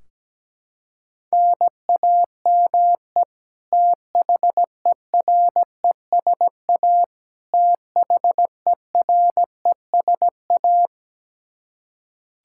Morse code, heard as a keyed tone, about 17 wpm, 700 Hz. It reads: NAME THERESA THERESA
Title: Practice QSO - Exchange 1 - Beginning - 17wpm